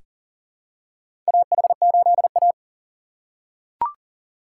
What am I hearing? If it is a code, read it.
Morse code, heard as a keyed tone, about 39 wpm, 700 Hz. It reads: AH8A
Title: USA Call Signs - 4 Characters - 2x Character Spacing - 40wpm